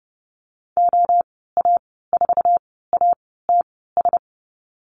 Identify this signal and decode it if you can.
Morse code, heard as a keyed tone, about 30 wpm, spaced as for 17 wpm, 700 Hz. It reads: OA4ATS
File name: Lesson 14g - Number 2 - Call Signs 30wpm